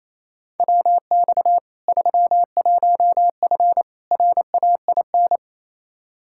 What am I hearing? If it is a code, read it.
Morse code, heard as a keyed tone, about 28 wpm, 700 Hz. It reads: WX 31F RAIN
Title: Practice QSO - Exchange 2 - Info - 28wpm